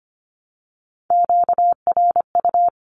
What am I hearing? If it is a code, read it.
Morse code, heard as a keyed tone, about 25 wpm, 700 Hz. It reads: QRU
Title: Q-Codes Repeated 3x 25wpm